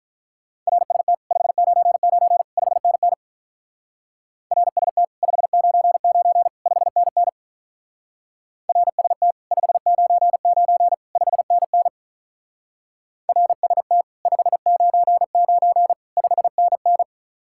Morse code, heard as a keyed tone, about 43 wpm, 700 Hz. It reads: RST 599 5NN RST 599 5NN RST 599 5NN RST 599 5NN
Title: Speed-Racing Common QSO Phrases - 35wpm